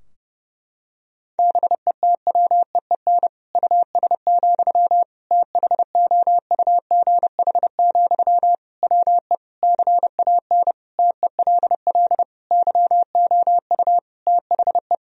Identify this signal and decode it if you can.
Morse code, heard as a keyed tone, about 30 wpm, 700 Hz. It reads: BETWEEN US, THOUGH, WE CAN TELL YOU THE
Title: Sentences from Top 1000 Words - Review Part 2 - 30wpm